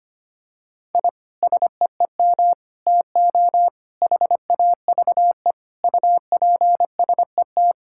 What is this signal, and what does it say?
Morse code, heard as a keyed tone, about 25 wpm, 700 Hz. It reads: I SEEM TO HAVE UPSET
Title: Sentences from Top 1000 Words - Review Part 5 - 25wpm